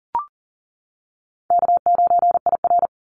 Morse code, heard as a keyed tone, about 40 wpm, 700 Hz. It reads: K9IR
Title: Call Signs - 40wpm